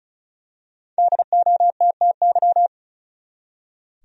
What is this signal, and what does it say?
Morse code, heard as a keyed tone, about 34 wpm, 700 Hz. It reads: DOTTY
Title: USA Common Names - 35wpm